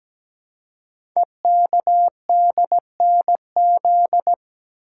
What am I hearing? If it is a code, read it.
Morse code, heard as a keyed tone, about 17 wpm, 700 Hz. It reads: EKDNZ
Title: Code Groups - 5 Letters - 17wpm